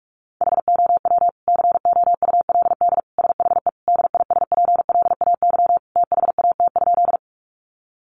Morse code, heard as a keyed tone, about 44 wpm, 700 Hz. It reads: HOW COULD SHE DISPLAY THAT?
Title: Sentences from Top 1200 Words 45wpm